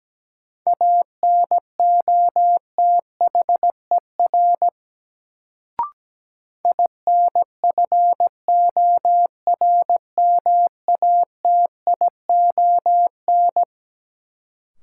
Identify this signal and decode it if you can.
Morse code, heard as a keyed tone, about 17 wpm, 700 Hz. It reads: ANOTHER INFORMATION
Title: Sentences from Top 600 Words - New Word Drill - 17wpm